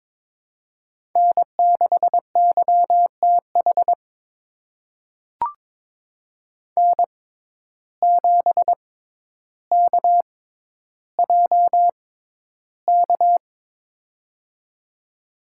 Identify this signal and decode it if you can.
Morse code, heard as a keyed tone, about 22 wpm, spaced as for 11 wpm, 700 Hz. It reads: N6YTH N7KJK
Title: USA Call Signs - 5 Characters - 6x Character Spacing - 22wpm